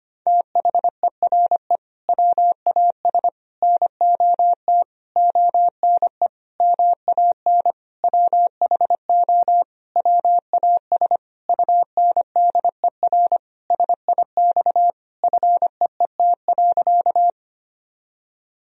Morse code, heard as a keyed tone, about 25 wpm, 700 Hz. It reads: THERE WAS NOT ONE MAN WHO WAS UNDER SIX FEET.
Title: Sentences from Top 1000 Words - Review Part 2 - 25wpm